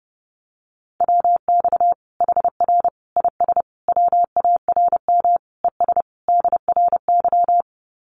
Morse code, heard as a keyed tone, about 30 wpm, 700 Hz. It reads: WX HR IS WARM ES DRY
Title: Common QSO Phrases 30wpm